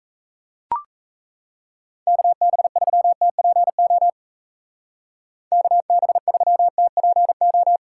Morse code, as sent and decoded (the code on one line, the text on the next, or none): -.- -... ...-- - .--. --- / -.- -... ...-- - .--. ---
KB3TPO KB3TPO